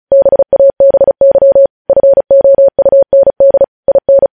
Morse code, heard as a keyed tone, about 36 wpm, 550 Hz. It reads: BABY FOUND IN
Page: News Headlines in Morse Code at 35 WPM